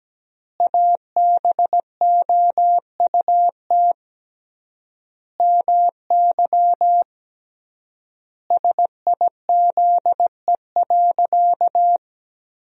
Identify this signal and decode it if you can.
Morse code, heard as a keyed tone, about 17 wpm, 700 Hz. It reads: ABOUT MY SIZE.
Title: 3x Word-Spacing - Sentences from Top 400 Words - 17wpm